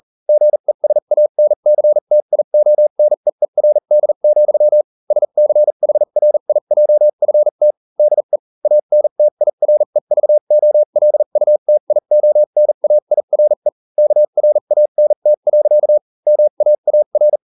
Morse code, heard as a keyed tone, about 40 wpm, spaced as for 35 wpm, 600 Hz. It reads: GESANCTIONEERD, SCHRIJFT DE ANTIREVOLUTIONAIRE KRANT. MAAR